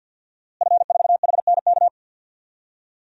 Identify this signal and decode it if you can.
Morse code, heard as a keyed tone, about 51 wpm, 700 Hz. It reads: F4HNK
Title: DXCC-100 Call Signs - 5 Characters - Repeated 2x - 50wpm